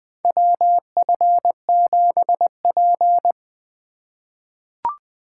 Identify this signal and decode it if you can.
Morse code, heard as a keyed tone, about 20 wpm, 700 Hz. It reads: WF7P
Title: USA Call Signs - 4 Characters - 3x Character Spacing - 20wpm